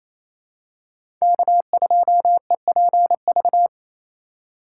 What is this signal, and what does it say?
Morse code, heard as a keyed tone, about 28 wpm, 700 Hz. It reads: K2EPV
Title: USA Call Signs - 5 Characters - 4x Character Spacing - 28wpm